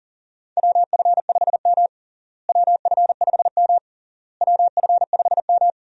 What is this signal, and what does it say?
Morse code, heard as a keyed tone, about 40 wpm, 700 Hz. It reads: WF5M WF5M WF5M
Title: USA Call Signs - 4 Characters - Repeated 3x - 40wpm